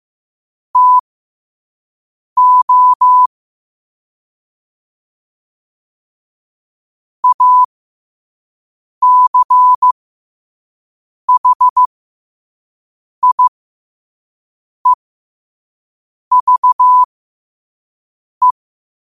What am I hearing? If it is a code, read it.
Morse code, heard as a keyed tone, about 15 wpm, spaced as for 5 wpm, 1000 Hz. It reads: TO ACHIEVE
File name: qotdfiles / 5wpm